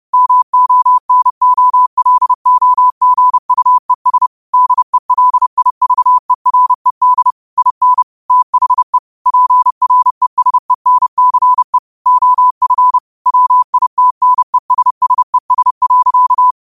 Morse code, heard as a keyed tone, about 30 wpm, 1000 Hz. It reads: MONOLOGUES DELIVERED IN THE PRESENCE OF WITNESSES.
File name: qotdfiles / 30wpm